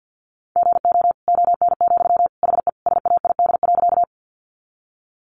Morse code, heard as a keyed tone, about 48 wpm, 700 Hz. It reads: GO ON, HE SAID.
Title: Sentences from Top 100 Words 50wpm